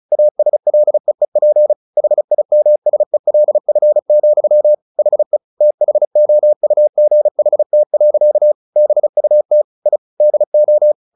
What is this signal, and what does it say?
Morse code, heard as a keyed tone, about 34 wpm, 600 Hz. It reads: ASLEEP HIMSELF, HE THOUGHT. BUT I DO
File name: the old man and the sea 2 35wpm